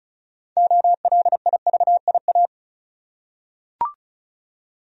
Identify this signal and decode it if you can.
Morse code, heard as a keyed tone, about 36 wpm, 700 Hz. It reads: OLIVIA
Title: USA Common Names - 35wpm